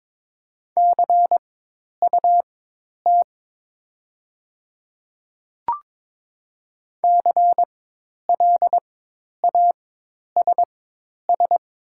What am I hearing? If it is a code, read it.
Morse code, heard as a keyed tone, about 22 wpm, spaced as for 10 wpm, 700 Hz. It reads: CUT CLASS
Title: 4x Character Spacing - Top 500 Words - 22wpm